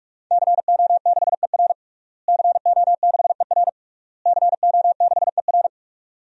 Morse code, heard as a keyed tone, about 46 wpm, 700 Hz. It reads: CO6ER CO6ER CO6ER
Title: DXCC-100 Call Signs - 5 Characters - Repeated 3x - 45wpm